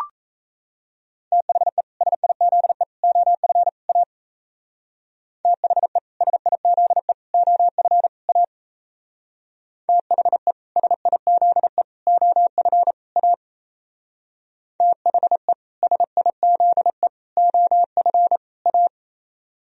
Morse code, heard as a keyed tone, about 33 wpm, 700 Hz. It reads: THE SIZE OF A THE SIZE OF A THE SIZE OF A THE SIZE OF A
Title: Speed-Racing - Sets of 4 Words 28wpm